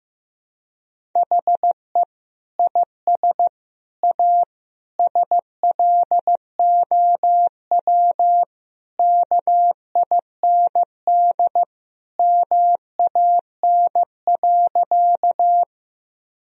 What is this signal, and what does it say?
Morse code, heard as a keyed tone, about 15 wpm, 700 Hz. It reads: HE IS A SLOW KIND MAN.
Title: Sentences from Top 1000 Words - Review Part 2 - 15wpm